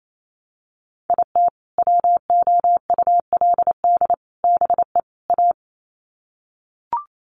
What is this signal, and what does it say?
Morse code, heard as a keyed tone, about 28 wpm, 700 Hz. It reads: IT WOULD BE A
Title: Sets of 4 Words 28wpm